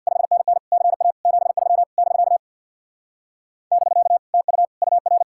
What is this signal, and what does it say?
Morse code, heard as a keyed tone, about 64 wpm, 700 Hz. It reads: 5NN CA 73 <BK> <BK> TU UR